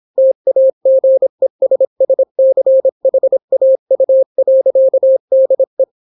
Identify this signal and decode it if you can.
Morse code, heard as a keyed tone, about 26 wpm, 530 Hz. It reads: TAGESSCHAU.DE